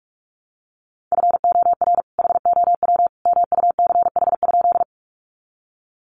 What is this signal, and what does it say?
Morse code, heard as a keyed tone, about 44 wpm, 700 Hz. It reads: FOR HOW MUCH?
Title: Sentences from Top 200 Words 45wpm